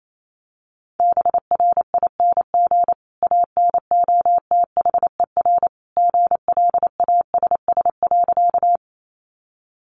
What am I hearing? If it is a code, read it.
Morse code, heard as a keyed tone, about 28 wpm, 700 Hz. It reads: BRING ANOTHER GLASS.
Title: Sentences from Top 1300 Words 28wpm